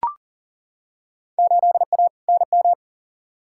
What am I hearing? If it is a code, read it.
Morse code, heard as a keyed tone, about 37 wpm, 700 Hz. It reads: 8A NM
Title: Field Day Exchanges Encore - 40wpm